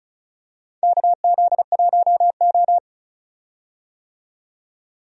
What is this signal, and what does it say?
Morse code, heard as a keyed tone, about 36 wpm, 700 Hz. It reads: KZ1O
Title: USA Call Signs - 4 Characters - 35wpm